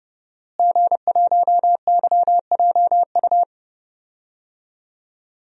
Morse code, heard as a keyed tone, about 30 wpm, 700 Hz. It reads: G1YJU
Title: DXCC-100 Call Signs - 5 Characters - Repeated 3x - 30wpm